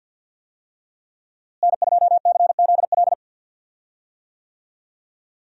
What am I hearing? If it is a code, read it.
Morse code, heard as a keyed tone, about 48 wpm, 700 Hz. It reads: N2CZL
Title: USA Call Signs - 5 Characters - Repeated 2x - 50wpm